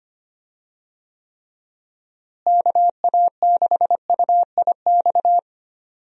Morse code, heard as a keyed tone, about 25 wpm, 700 Hz. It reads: KA6UIX
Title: USA Call Signs - 6 Characters - Repeated 2x - 25wpm